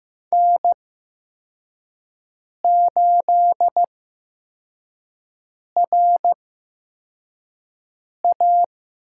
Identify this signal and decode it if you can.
Morse code, heard as a keyed tone, about 15 wpm, spaced as for 4 wpm, 700 Hz. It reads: N8RA
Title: USA Call Signs - 5 Characters - 8x Character Spacing - 15wpm